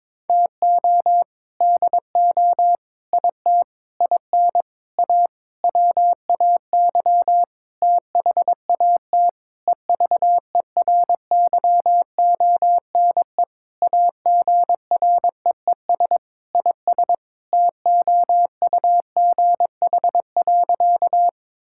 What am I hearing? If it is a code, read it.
Morse code, heard as a keyed tone, about 21 wpm, 700 Hz. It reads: TO DO IT IN A WAY THAT EVERYONE AGREES IS TOUGH.
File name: Sentences with 7-Word Phrases - 22wpm